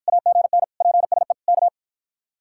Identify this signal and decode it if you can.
Morse code, heard as a keyed tone, about 54 wpm, 700 Hz. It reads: AGN PSE K